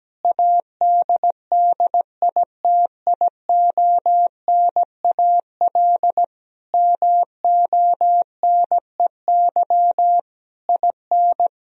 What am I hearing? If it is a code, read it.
Morse code, heard as a keyed tone, about 17 wpm, 700 Hz. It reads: ADDITIONAL MONEY IN
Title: Sentences from Top 1000 Words - Review Part 4 - 17wpm